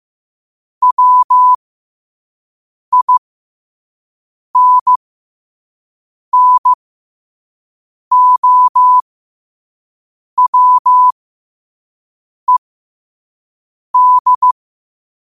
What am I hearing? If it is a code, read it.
Morse code, heard as a keyed tone, about 15 wpm, spaced as for 5 wpm, 1000 Hz. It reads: WINNOWED
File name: qotdfiles / 5wpm